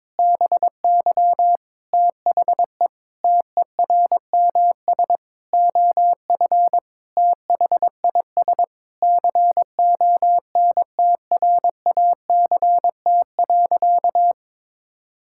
Morse code, heard as a keyed tone, about 22 wpm, 700 Hz. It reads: BY THE TERMS OF THIS CONTRACT.
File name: Sentences with 9-Word Phrases - 22wpm